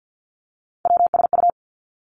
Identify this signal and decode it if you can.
Morse code, heard as a keyed tone, about 50 wpm, 700 Hz. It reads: WSU
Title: Three Challenging Letter-Number Combinations 50wpm